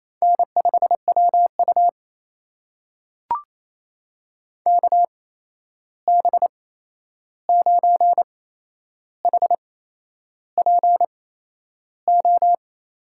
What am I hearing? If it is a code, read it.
Morse code, heard as a keyed tone, about 28 wpm, spaced as for 10 wpm, 700 Hz. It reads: N5WU KB9HPO
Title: USA Call Signs - 8x Character Spacing - 28wpm